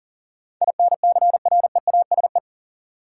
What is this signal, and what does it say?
Morse code, heard as a keyed tone, about 40 wpm, 700 Hz. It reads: INCREASE
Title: Sets of 5 Words - New Word Drill - 40wpm